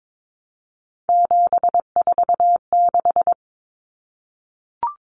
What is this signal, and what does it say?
Morse code, heard as a keyed tone, about 22 wpm, 700 Hz. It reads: 746
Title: Common Cut Numbers - 3-Digit 22wpm